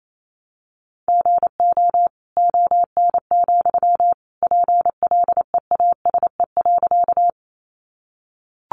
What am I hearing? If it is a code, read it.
Morse code, heard as a keyed tone, about 28 wpm, 700 Hz. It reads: GO ON, PLEASE.
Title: Sentences from Top 1500 Words 28wpm